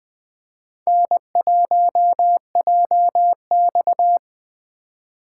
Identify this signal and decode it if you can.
Morse code, heard as a keyed tone, about 20 wpm, 700 Hz. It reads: N1JX
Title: USA Call Signs - 4 Characters - 4x Character Spacing - 20wpm